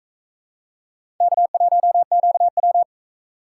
Morse code, heard as a keyed tone, about 42 wpm, 700 Hz. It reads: K1QW